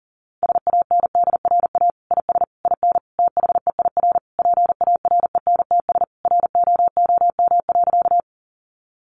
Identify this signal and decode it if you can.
Morse code, heard as a keyed tone, about 37 wpm, 700 Hz. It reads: SANDRA IS IN THEIR PARENTS ROOM.
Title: Easy Phrases 40wpm